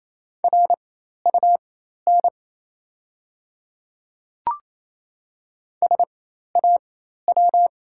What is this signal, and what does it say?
Morse code, heard as a keyed tone, about 28 wpm, spaced as for 13 wpm, 690 Hz. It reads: RUN SAW
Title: 4x Character Spacing - Top 300 Words - 28wpm